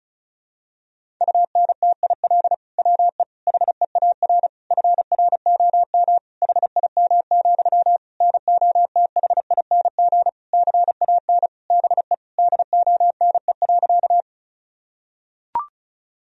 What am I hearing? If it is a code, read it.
Morse code, heard as a keyed tone, about 36 wpm, 700 Hz. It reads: UNTIL WE HEAR FROM HIM, NOTHING CAN BE DONE.
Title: Sentences from Top 1000 Words - Review Part 1 - 35wpm